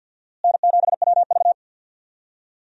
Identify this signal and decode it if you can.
Morse code, heard as a keyed tone, about 51 wpm, 700 Hz. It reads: N7WV
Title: USA Call Signs - 4 Characters - Repeated 2x - 50wpm